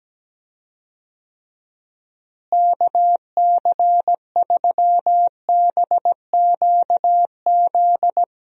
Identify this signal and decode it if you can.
Morse code, heard as a keyed tone, about 17 wpm, 700 Hz. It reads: KC3BQZ
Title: USA Call Signs - 6 Characters - Repeated 2x - 17wpm